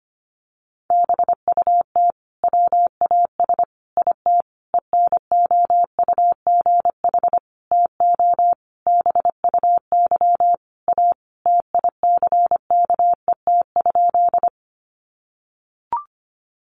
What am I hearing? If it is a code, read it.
Morse code, heard as a keyed tone, about 25 wpm, 700 Hz. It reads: BUT WAS IT ENOUGH TO BUY A TICKET?
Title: Sentences from Top 1400 Words 25wpm